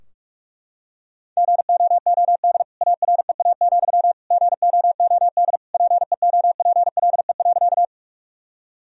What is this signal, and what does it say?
Morse code, heard as a keyed tone, about 41 wpm, 700 Hz. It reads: GOOD AREA, GOOD PEOPLE.
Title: Sentences from Top 1000 Words - Review Part 2 - 45wpm